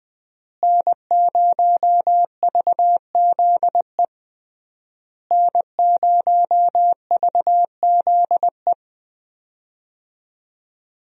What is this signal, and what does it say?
Morse code, heard as a keyed tone, about 20 wpm, 700 Hz. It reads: N0VZE N0VZE
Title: USA Call Signs - 5 Characters - Repeated 3x - 20wpm